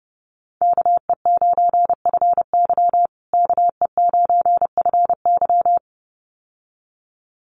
Morse code, heard as a keyed tone, about 30 wpm, 700 Hz. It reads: KE9FY KE9FY
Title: USA Call Signs Repeated 2x 30wpm